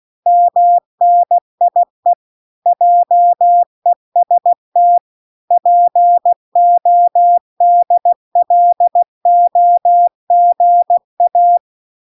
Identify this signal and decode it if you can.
Morse code, heard as a keyed tone, about 16 wpm, 700 Hz. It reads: MNIE JEST PODLOGA